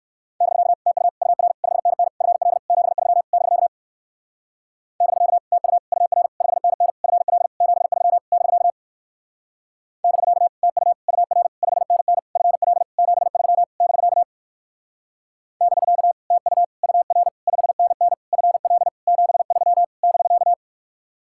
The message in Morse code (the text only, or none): <BK> TU UR 5NN FL 73 <BK> <BK> TU UR 5NN FL 73 <BK> <BK> TU UR 5NN FL 73 <BK> <BK> TU UR 5NN FL 73 <BK>